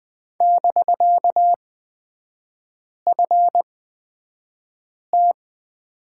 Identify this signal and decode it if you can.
Morse code, heard as a keyed tone, about 20 wpm, spaced as for 5 wpm, 700 Hz. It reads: <BK>FT
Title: 250 Lesson 40e - BK - Three Characters 20wpm